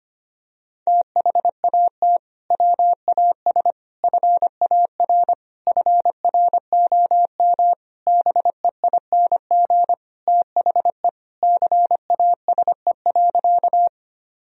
Morse code, heard as a keyed tone, about 25 wpm, 700 Hz. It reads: THAT WAS FAR FROM BEING THE CASE.
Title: Sentences from Top 1000 Words - Review Part 3 - 25wpm